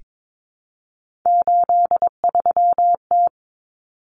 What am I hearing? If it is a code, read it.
Morse code, heard as a keyed tone, about 22 wpm, 700 Hz. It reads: 83T
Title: Common Cut Numbers - 3-Digit 22wpm